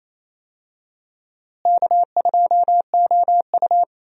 Morse code, heard as a keyed tone, about 28 wpm, 700 Hz. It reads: K2OU
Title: USA Call Signs - 4 Characters - Repeated 3x - 28wpm